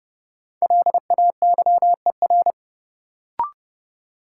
Morse code, heard as a keyed tone, about 30 wpm, 700 Hz. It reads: LAYER
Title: Sentences from Top 1600 Words - New Word Drill - 30wpm